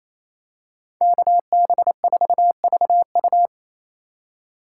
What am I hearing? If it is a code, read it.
Morse code, heard as a keyed tone, about 28 wpm, 700 Hz. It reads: KB4VU
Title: K1USN SST QSO Practice - 28wpm